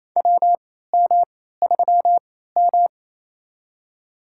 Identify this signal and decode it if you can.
Morse code, heard as a keyed tone, about 28 wpm, spaced as for 16 wpm, 700 Hz. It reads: WM3M
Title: USA Call Signs - 4 Characters - 3x Character Spacing - 28wpm